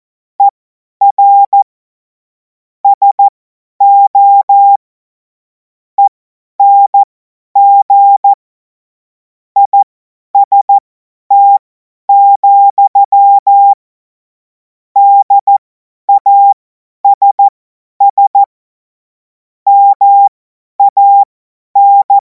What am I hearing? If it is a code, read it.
Morse code, heard as a keyed tone, about 14 wpm, spaced as for 10 wpm, 800 Hz. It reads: ER SO ENG IST, DASS MAN
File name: Morsekurs / Witze / Witz 02. Juni 2026 50 F70